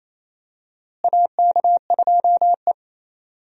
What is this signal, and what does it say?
Morse code, heard as a keyed tone, about 28 wpm, 700 Hz. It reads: AK2E
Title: USA Call Signs - 4 Characters - Repeated 2x - 28wpm